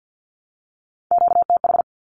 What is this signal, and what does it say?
Morse code, heard as a keyed tone, about 48 wpm, 700 Hz. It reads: QTH
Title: Q-Codes 50wpm